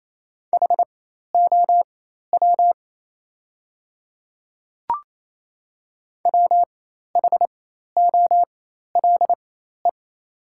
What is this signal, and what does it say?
Morse code, heard as a keyed tone, about 29 wpm, spaced as for 13 wpm, 690 Hz. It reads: HOW WHOLE
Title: 4x Character Spacing - Top 500 Words - 28wpm